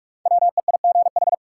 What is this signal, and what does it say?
Morse code, heard as a keyed tone, about 42 wpm, 700 Hz. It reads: WEIGH
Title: Five Letter Words - 45wpm